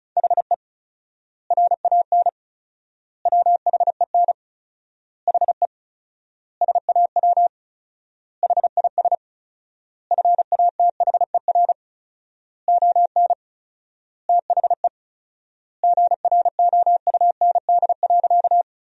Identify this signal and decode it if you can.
Morse code, heard as a keyed tone, about 36 wpm, 700 Hz. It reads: HE RAN WHEN HE SAW HIS FATHER ON THE GROUND.
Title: 4x Word-Spacing - Sentences from Top 500 Words - 35wpm